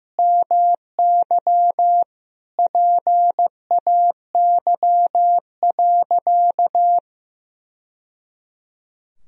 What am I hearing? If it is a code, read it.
Morse code, heard as a keyed tone, about 15 wpm, 700 Hz. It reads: MY PAY.
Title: Sentences from Top 1000 Words - Review Part 4 - 15wpm